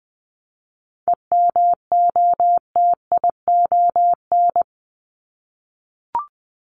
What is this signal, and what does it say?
Morse code, heard as a keyed tone, about 20 wpm, 700 Hz. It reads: EMOTION
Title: Top 1501-1750 Words 20wpm